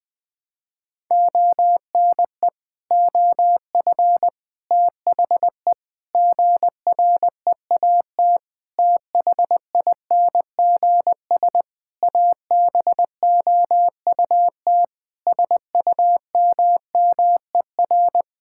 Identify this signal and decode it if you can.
Morse code, heard as a keyed tone, about 21 wpm, 700 Hz. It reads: ONE OF THE GREAT THINGS ABOUT SUMMER
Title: Sentences with 6-Word Phrases - Spoken Beginning - 20wpm